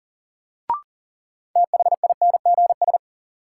Morse code, heard as a keyed tone, about 40 wpm, 700 Hz. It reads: THINGS
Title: Sentences from Top 1000 Words - Review Part 4 - New Word Drill - 40wpm